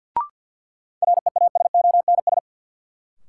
Morse code, heard as a keyed tone, about 53 wpm, 700 Hz. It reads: REASONS
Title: Sets of 4 Words - New Word Drill - 50wpm